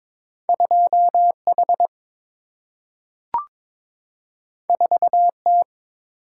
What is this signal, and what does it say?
Morse code, heard as a keyed tone, about 22 wpm, 700 Hz. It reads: 2H 4T
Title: Single-Letter Single-Number 22wpm